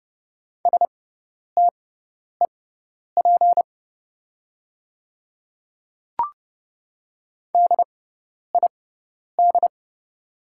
Morse code, heard as a keyed tone, about 30 wpm, spaced as for 10 wpm, 690 Hz. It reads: STEP DID
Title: 6x Character Spacing - Top 500 Words - 30wpm